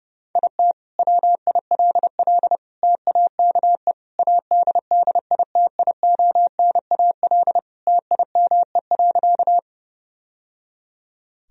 Morse code, heard as a keyed tone, about 30 wpm, 700 Hz. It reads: IT WILL TAKE ADDITIONAL TIME.
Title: Sentences from Top 1000 Words - Review Part 4 - 30wpm